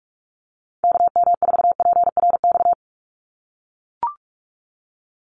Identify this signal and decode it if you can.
Morse code, heard as a keyed tone, about 44 wpm, 690 Hz. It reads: KM4PRX
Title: USA Call Signs 45wpm